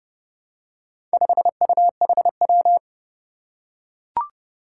Morse code, heard as a keyed tone, about 30 wpm, 700 Hz. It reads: 5UHW
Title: Four Challenging Letter-Number Combinations 30wpm